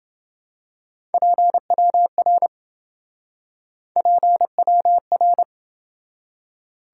Morse code, heard as a keyed tone, about 28 wpm, 700 Hz. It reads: PWR PWR